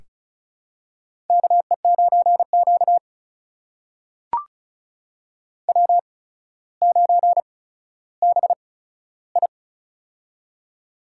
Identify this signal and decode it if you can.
Morse code, heard as a keyed tone, about 33 wpm, spaced as for 15 wpm, 700 Hz. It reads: KE9Q W9BI
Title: USA Call Signs - 4 Characters - 8x Character Spacing - 35wpm